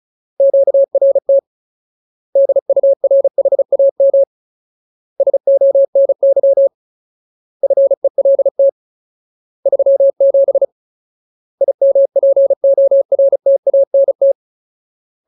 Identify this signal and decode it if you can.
Morse code, heard as a keyed tone, about 34 wpm, 550 Hz. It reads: QRT DURHAM SONY FELT 37 IMPORTANT
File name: Ultimate Word List / 35WPM Ultimate